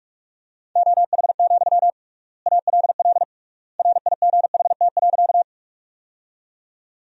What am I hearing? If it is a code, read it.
Morse code, heard as a keyed tone, about 44 wpm, 700 Hz. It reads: OH, ALL RIGHT.
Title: 3x Word-Spacing - Sentences from Top 500 Words - 45wpm